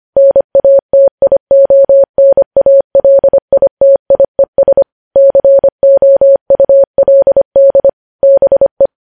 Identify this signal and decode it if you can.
Morse code, heard as a keyed tone, about 25 wpm, 550 Hz. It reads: NATIONALITIES COULD BE
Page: news headlines in morse code at 25 wpm